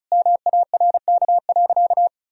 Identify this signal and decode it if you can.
Morse code, heard as a keyed tone, about 34 wpm, 700 Hz. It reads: MARK.
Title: Sentences from Top 1000 Words - Review Part 2 - 35wpm